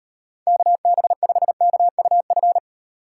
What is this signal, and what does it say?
Morse code, heard as a keyed tone, about 37 wpm, 700 Hz. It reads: KB5KUF